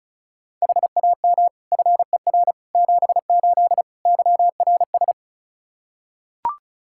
Morse code, heard as a keyed tone, about 36 wpm, 700 Hz. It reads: HAM FER 78 YRS
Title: Practice QSO - Exchange 3 - More Info - 35wpm